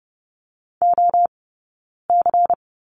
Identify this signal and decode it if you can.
Morse code, heard as a keyed tone, about 30 wpm, 700 Hz. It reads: O C
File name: Lesson 16d - Letter c - Two Characters 30wpm